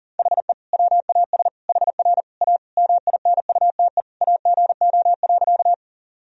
Morse code, heard as a keyed tone, about 37 wpm, 700 Hz. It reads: HE WAS HR A MINUTE AGO.
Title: Sentences from Top 500 Words and Abbreviations 40wpm